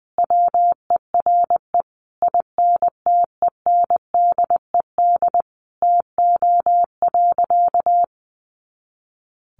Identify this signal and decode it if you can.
Morse code, heard as a keyed tone, about 20 wpm, 700 Hz. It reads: WERE INTENDED TO.
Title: A Sound Of Thunder 20wpm